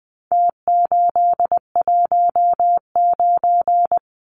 Morse code, heard as a keyed tone, about 20 wpm, 700 Hz. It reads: T819
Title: Numbers - Contest Style - 4-Digit 20wpm